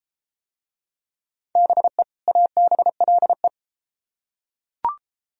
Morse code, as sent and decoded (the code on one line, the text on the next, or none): -... / .- -... .-.. .
BE ABLE